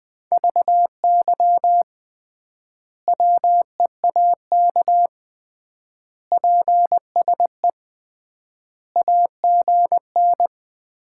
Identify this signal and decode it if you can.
Morse code, heard as a keyed tone, about 20 wpm, 700 Hz. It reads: VY WEAK PSE AGN
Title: Common QSO Phrases - 3x Word-Spacing - 20wpm